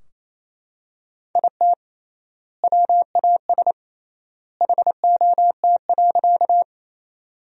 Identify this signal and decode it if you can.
Morse code, heard as a keyed tone, about 28 wpm, 700 Hz. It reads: IT WAS HOT.
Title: 3x Word-Spacing - Sentences from Top 500 Words - 28wpm